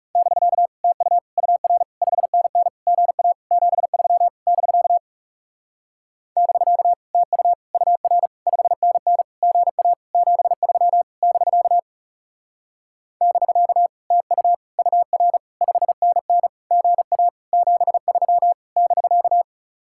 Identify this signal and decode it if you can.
Morse code, heard as a keyed tone, about 39 wpm, 700 Hz. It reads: <BK> TU UR 5NN GA 73 <BK> <BK> TU UR 5NN GA 73 <BK> <BK> TU UR 5NN GA 73 <BK>